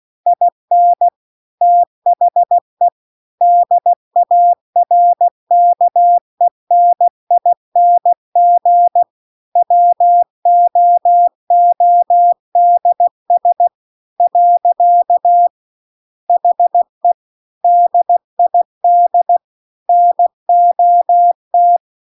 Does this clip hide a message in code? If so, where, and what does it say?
Yes: Morse code from 0.3 to 21.8 s, heard as a keyed tone, about 16 wpm, 700 Hz. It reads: IN THE DARKENING WOODS . HE DID NOT